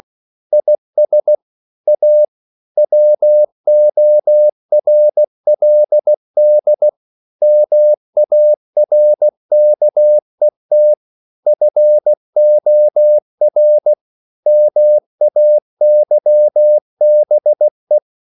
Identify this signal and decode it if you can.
Morse code, heard as a keyed tone, about 16 wpm, 600 Hz. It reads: IS A WORLD MARKET FOR MAYBE
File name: CW / Practice / EnglishQuotations0416